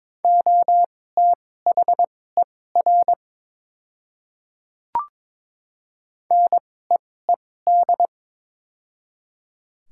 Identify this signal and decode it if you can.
Morse code, heard as a keyed tone, about 22 wpm, spaced as for 16 wpm, 700 Hz. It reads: OTHER NEED
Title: 2x Character Spacing - Top 200 Words - 22wpm